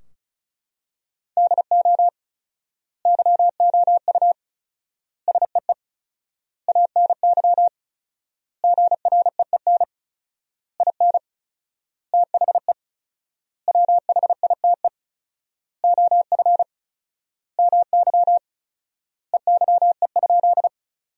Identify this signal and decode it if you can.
Morse code, heard as a keyed tone, about 34 wpm, 700 Hz. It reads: DO YOU SEE ANY GREEN IN THE WHITE OF MY EYE?
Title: 4x Word-Spacing - Sentences from Top 500 Words - 35wpm